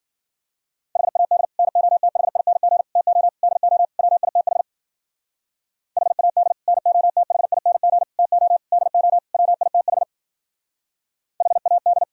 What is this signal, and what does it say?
Morse code, heard as a keyed tone, about 52 wpm, 700 Hz. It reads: HAD NOTHING TO DO WITH HAD NOTHING TO DO WITH HAD